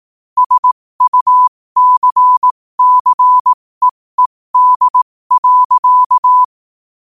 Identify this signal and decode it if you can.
Morse code, heard as a keyed tone, about 18 wpm, spaced as for 15 wpm, 1000 Hz. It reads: SUCCEED.
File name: qotdfiles / 15wpm